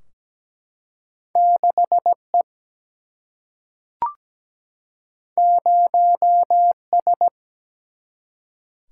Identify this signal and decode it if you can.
Morse code, heard as a keyed tone, about 17 wpm, 700 Hz. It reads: 6E 0S
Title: Single-Letter Single-Number 17wpm